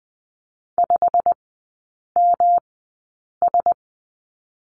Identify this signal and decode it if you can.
Morse code, heard as a keyed tone, about 20 wpm, spaced as for 8 wpm, 690 Hz. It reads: 5MS
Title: 172 Lesson 28e - Number 9 - Three Characters 20wpm